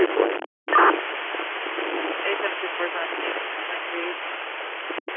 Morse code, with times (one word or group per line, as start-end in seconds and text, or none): none